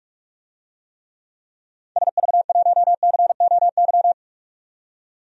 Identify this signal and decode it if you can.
Morse code, heard as a keyed tone, about 46 wpm, 700 Hz. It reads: SV1COY
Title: DXCC-100 Call Signs - 6 Characters - 45wpm